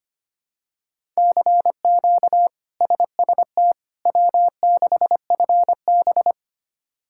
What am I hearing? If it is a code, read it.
Morse code, heard as a keyed tone, about 25 wpm, 700 Hz. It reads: CQ SST W6FB
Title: K1USN SST QSO Practice - 25wpm